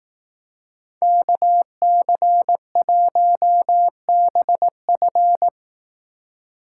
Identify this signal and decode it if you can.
Morse code, heard as a keyed tone, about 18 wpm, 700 Hz. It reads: KC1BF
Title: Speed Racing - USA 2x2 Call Signs 15wpm